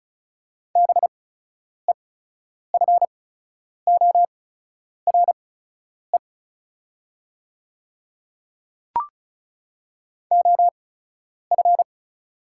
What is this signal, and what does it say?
Morse code, heard as a keyed tone, about 34 wpm, spaced as for 10 wpm, 700 Hz. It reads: BEFORE OF
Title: 8x Character Spacing - Top 300 Words - 35wpm